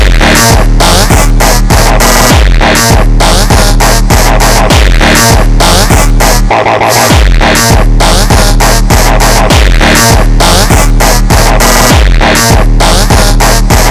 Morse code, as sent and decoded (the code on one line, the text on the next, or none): none